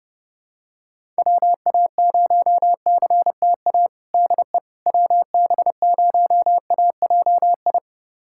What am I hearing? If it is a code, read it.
Morse code, heard as a keyed tone, about 30 wpm, 700 Hz. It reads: WA0CTA DE WB0AJI
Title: USA Callsign DE Callsign - 6 Characters - 30wpm